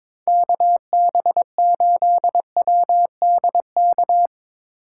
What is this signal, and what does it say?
Morse code, heard as a keyed tone, about 24 wpm, 700 Hz. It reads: KB8WDK
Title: Speed Racing - USA 2x3 Call Signs 22wpm